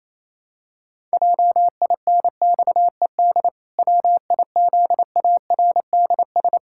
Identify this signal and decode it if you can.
Morse code, heard as a keyed tone, about 28 wpm, 700 Hz. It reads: JINXED WIZARDS
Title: Pangram Challenge v2 - 28wpm